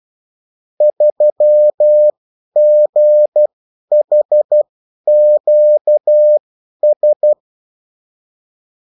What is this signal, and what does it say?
Morse code, heard as a keyed tone, about 12 wpm, spaced as for 10 wpm, 600 Hz. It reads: 3GHQS
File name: morse mp3 / revision9-vitesse 10 mots